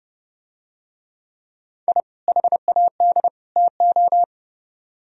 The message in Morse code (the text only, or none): I HAD TO